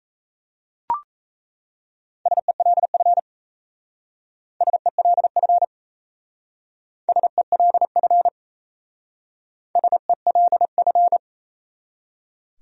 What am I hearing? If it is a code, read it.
Morse code, heard as a keyed tone, about 34 wpm, 700 Hz. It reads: SELF SELF SELF SELF
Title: Speed-Racing Top 500 Words 28wpm